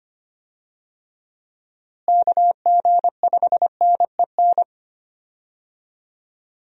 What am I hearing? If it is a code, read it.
Morse code, heard as a keyed tone, about 25 wpm, 700 Hz. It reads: KG5NEN
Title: USA Call Signs - 6 Characters - Repeated 2x - 25wpm